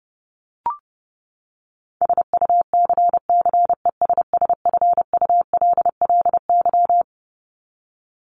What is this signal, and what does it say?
Morse code, heard as a keyed tone, about 28 wpm, 700 Hz. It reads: SUCCESSFULLY
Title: LY Suffix - Top 100 Words with Base Words 30wpm